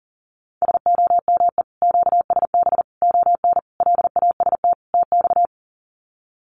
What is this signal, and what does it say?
Morse code, heard as a keyed tone, about 40 wpm, 700 Hz. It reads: SOME QSB ON LAST TX
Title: Common QSO Phrases 40wpm